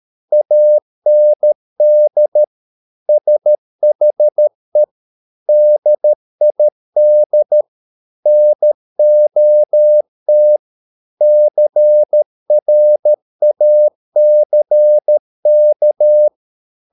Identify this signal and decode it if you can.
Morse code, heard as a keyed tone, about 13 wpm, 600 Hz. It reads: AND SHE DID NOT CRACK